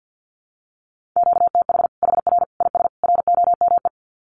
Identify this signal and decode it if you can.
Morse code, heard as a keyed tone, about 50 wpm, 700 Hz. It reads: QTH HR IS ROME
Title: Common QSO Phrases 50wpm